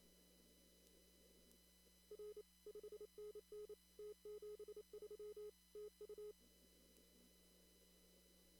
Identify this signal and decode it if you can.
Morse code, heard as a keyed tone, about 27 wpm, 420 Hz. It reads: R 5NN T73 TU
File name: CW / kt5j wpx cw-op-k5pi